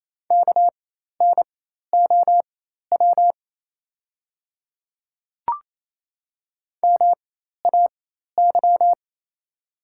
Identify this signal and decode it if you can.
Morse code, heard as a keyed tone, about 28 wpm, spaced as for 13 wpm, 700 Hz. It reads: KNOW MAY
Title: 4x Character Spacing - Top 100 Words - 28wpm